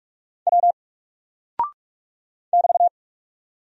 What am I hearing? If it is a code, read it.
Morse code, heard as a keyed tone, about 46 wpm, 700 Hz. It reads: W <BT>
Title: Single Character (Letter:Number:Punctuation:Prosign) Alt 45wpm